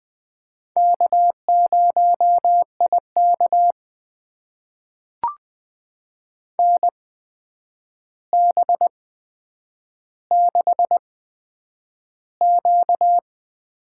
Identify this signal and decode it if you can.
Morse code, heard as a keyed tone, about 20 wpm, spaced as for 9 wpm, 700 Hz. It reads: K0IK NB6Q
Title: USA Call Signs - 4 Characters - 8x Character Spacing - 20wpm